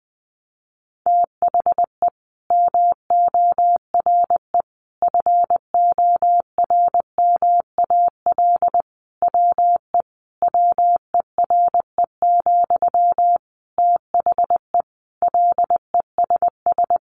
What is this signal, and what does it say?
Morse code, heard as a keyed tone, about 20 wpm, 700 Hz. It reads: THE MORE FORMAL WE WERE, THE LESS
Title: Sentences from Top 1300 Words 20wpm